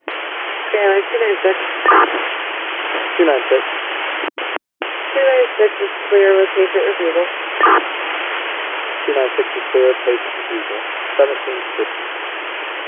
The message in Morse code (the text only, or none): none